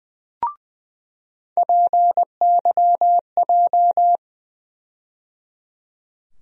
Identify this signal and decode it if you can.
Morse code, heard as a keyed tone, about 20 wpm, 700 Hz. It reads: PYJ
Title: USA Call Sign Postfixes - 3 Letters 20wpm